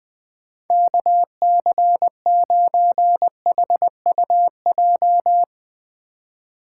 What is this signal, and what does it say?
Morse code, heard as a keyed tone, about 20 wpm, 700 Hz. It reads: KC9HUJ
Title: USA Call Signs - 2x Character Spacing - 20wpm